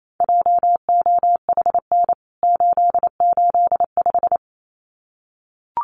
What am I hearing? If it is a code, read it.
Morse code, heard as a keyed tone, about 28 wpm, 700 Hz. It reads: JOHN 885
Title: CWT Exchange 28wpm